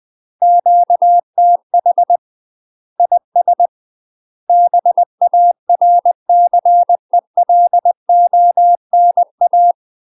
Morse code, heard as a keyed tone, about 20 wpm, 700 Hz. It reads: QTH IS BARCELONA
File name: UK MASTER QSOs TRAINING / 20WPM / UK MASTER QSOs 0066